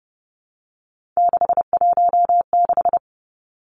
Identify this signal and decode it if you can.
Morse code, heard as a keyed tone, about 30 wpm, 700 Hz. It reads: 616
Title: Three Challenging Letter-Number Combinations 30wpm